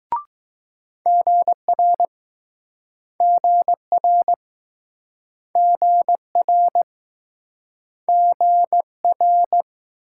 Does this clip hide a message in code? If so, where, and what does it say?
Yes: Morse code from 1.1 to 9.6 s, heard as a keyed tone, about 21 wpm, 700 Hz. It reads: GR GR GR GR